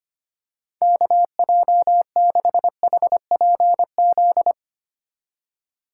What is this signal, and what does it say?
Morse code, heard as a keyed tone, about 25 wpm, 700 Hz. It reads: KJ6HPZ
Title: USA Call Signs - 6 Characters - 3x Character Spacing - 25wpm